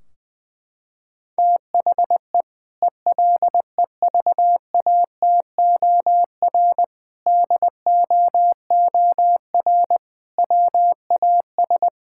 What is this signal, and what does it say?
Morse code, heard as a keyed tone, about 20 wpm, 700 Hz. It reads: THE ELEVATOR DOOR WAS
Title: Sentences from Top 1600 Words 20wpm